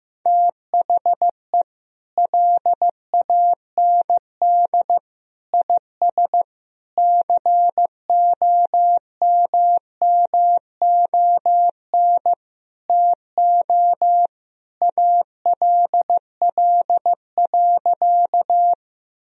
Morse code, heard as a keyed tone, about 16 wpm, 700 Hz. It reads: THE LAND IS COMMON TO ALL.
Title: Sentences from Top 1000 Words - Review Part 3 - 15wpm